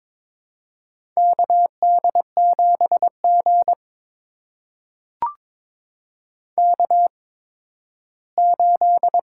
Morse code, heard as a keyed tone, about 21 wpm, spaced as for 13 wpm, 700 Hz. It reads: KD7G K8
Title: USA Call Signs - 8x Character Spacing - 22wpm